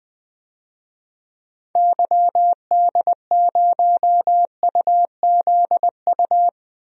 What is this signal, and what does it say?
Morse code, heard as a keyed tone, about 20 wpm, 700 Hz. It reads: YD0UZU
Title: DXCC-100 Call Signs - 6 Characters - Repeated 3x - 20wpm